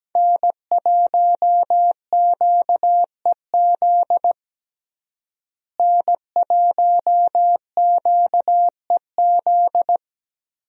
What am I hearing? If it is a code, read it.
Morse code, heard as a keyed tone, about 17 wpm, 700 Hz. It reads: N1QEZ N1QEZ
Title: USA Call Signs - 5 Characters - Repeated 3x - 17wpm